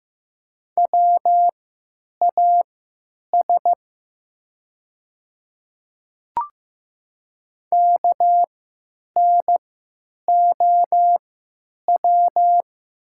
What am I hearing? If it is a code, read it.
Morse code, heard as a keyed tone, about 15 wpm, spaced as for 9 wpm, 700 Hz. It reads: WAS KNOW
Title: 3x Character Spacing - Top 100 Words - 15wpm